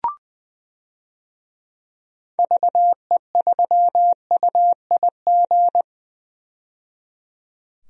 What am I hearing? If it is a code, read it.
Morse code, heard as a keyed tone, about 21 wpm, 700 Hz. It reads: VE3UIG
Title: DXCC-100 Call Signs - 6 Characters - 20wpm